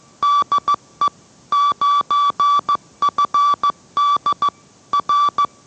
Morse code, heard as a keyed tone, about 15 wpm, 1160 Hz. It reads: DE 9FD R